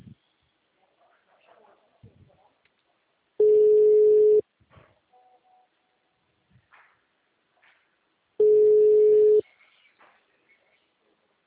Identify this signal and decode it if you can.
Ringback tone (Europe, 425 Hz), with speech.
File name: out- -2001- - - .56